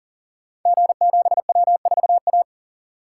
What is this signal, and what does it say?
Morse code, heard as a keyed tone, about 39 wpm, 700 Hz. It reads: G7W4A
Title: Code Groups - 5 Letter:Numbers - 40wpm